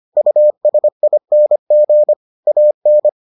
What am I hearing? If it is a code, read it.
Morse code, heard as a keyed tone, about 25 wpm, 600 Hz. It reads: USING AN